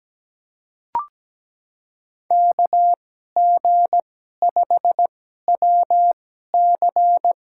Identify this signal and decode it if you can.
Morse code, heard as a keyed tone, about 17 wpm, spaced as for 12 wpm, 700 Hz. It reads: KG5WC
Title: USA Call Signs - 2x Character Spacing - 17wpm